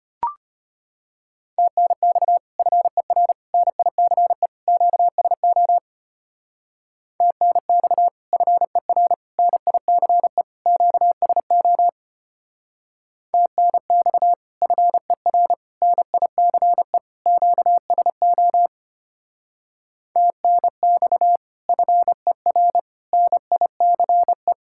Morse code, heard as a keyed tone, about 30 wpm, 700 Hz. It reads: TNX FER NICE QSO TNX FER NICE QSO TNX FER NICE QSO TNX FER NICE
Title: Speed-Racing Common QSO Phrases Encore - 25wpm